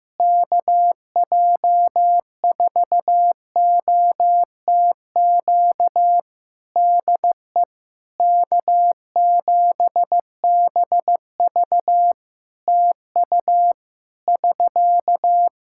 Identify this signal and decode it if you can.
Morse code, heard as a keyed tone, about 15 wpm, 700 Hz. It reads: KJ4OTQ DE K7BV TU <SK>
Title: K1USN SST QSO Practice 15wpm